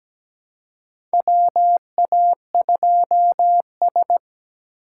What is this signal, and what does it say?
Morse code, heard as a keyed tone, about 17 wpm, 700 Hz. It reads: WA2S
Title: USA Call Signs - 4 Characters - 4x Character Spacing - 17wpm